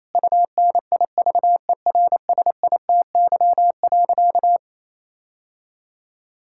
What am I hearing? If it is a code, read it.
Morse code, heard as a keyed tone, about 28 wpm, 700 Hz. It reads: UNIVERSITY.
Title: Sentences from Top 1000 Words - Review Part 6 - 28wpm